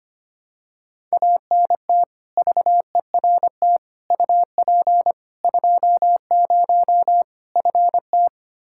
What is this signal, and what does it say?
Morse code, heard as a keyed tone, about 25 wpm, 700 Hz. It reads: ANT VERT UP 20 FT
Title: Practice QSO - Exchange 2 - Info - 25wpm